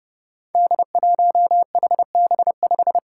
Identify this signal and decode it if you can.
Morse code, heard as a keyed tone, about 30 wpm, 700 Hz. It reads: D1HB5
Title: Five Challenging Letter-Number Combinations 30wpm